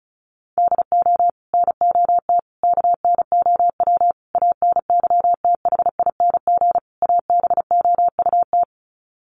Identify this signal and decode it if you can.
Morse code, heard as a keyed tone, about 34 wpm, 700 Hz. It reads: DO NOT KNOW ANYTHING ABOUT
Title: Sets of 5 Words 35wpm